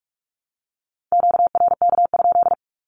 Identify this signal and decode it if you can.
Morse code, heard as a keyed tone, about 46 wpm, 700 Hz. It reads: QRK?
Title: Q-Codes with Spoken Meaning - 45wpm